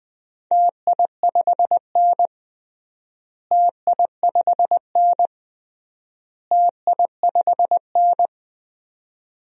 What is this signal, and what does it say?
Morse code, heard as a keyed tone, about 20 wpm, 700 Hz. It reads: TI5N TI5N TI5N
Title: DXCC-100 Call Signs - 4 Characters - Repeated 3x - 20wpm